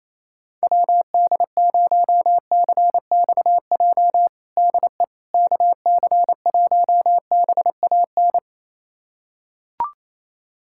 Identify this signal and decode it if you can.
Morse code, heard as a keyed tone, about 28 wpm, 700 Hz. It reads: WD0CXJ DE KC1BAN
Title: USA Callsign DE Callsign - 6 Characters - 28wpm